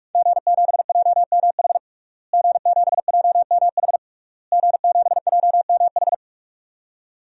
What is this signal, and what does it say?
Morse code, heard as a keyed tone, about 41 wpm, 700 Hz. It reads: G7JMH G7JMH G7JMH
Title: DXCC-100 Call Signs - 5 Characters - Repeated 3x - 45wpm